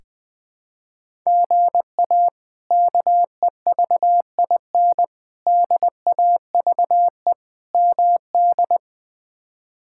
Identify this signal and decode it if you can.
Morse code, heard as a keyed tone, about 21 wpm, 700 Hz. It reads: GA KEVIN DAVE MD
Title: K1USN SST QSO Practice - 20wpm